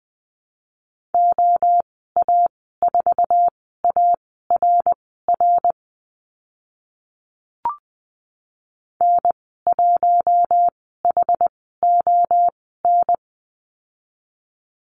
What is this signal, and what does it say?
Morse code, heard as a keyed tone, about 20 wpm, spaced as for 14 wpm, 700 Hz. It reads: OA4ARR N1HON
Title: 082 Lesson 14g - Number 2 - Call Signs 20wpm